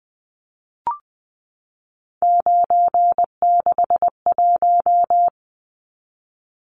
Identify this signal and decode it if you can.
Morse code, heard as a keyed tone, about 20 wpm, 700 Hz. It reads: 961
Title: Numbers - 3-Digit 20wpm